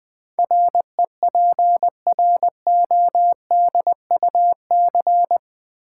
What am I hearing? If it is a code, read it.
Morse code, heard as a keyed tone, about 20 wpm, 700 Hz. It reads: REPRODUC
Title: Build A Word - 4 Letters At A Time - 12-Letter Words - 20wpm